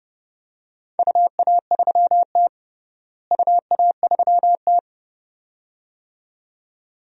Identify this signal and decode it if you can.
Morse code, heard as a keyed tone, about 32 wpm, 700 Hz. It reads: UA3T UA3T
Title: DXCC-100 Call Signs - 4 Characters - Repeated 2x - 30wpm